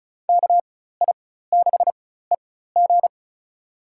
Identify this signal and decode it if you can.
Morse code, heard as a keyed tone, about 36 wpm, spaced as for 16 wpm, 700 Hz. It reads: KI6EG
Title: USA Call Signs - 4x Character Spacing - 35wpm